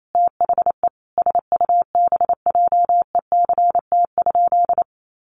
Morse code, heard as a keyed tone, about 28 wpm, 700 Hz. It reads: THE SUBJECT?
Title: Sentences from Top 800 Words 28wpm